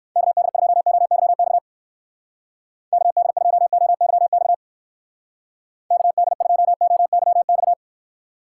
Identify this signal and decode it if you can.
Morse code, heard as a keyed tone, about 59 wpm, 700 Hz. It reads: KD2OYX KD2OYX KD2OYX